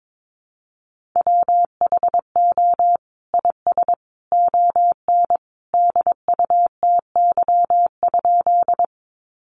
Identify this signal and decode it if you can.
Morse code, heard as a keyed tone, about 22 wpm, 700 Hz. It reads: WHO IS ON DUTY?
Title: Sentences from Top 1500 Words 22wpm